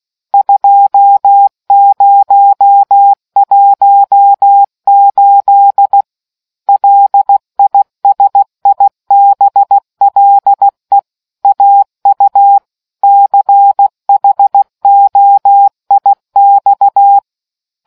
Morse code, heard as a keyed tone, about 16 wpm, 800 Hz. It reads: 2018 LISIBLE AU CHOIX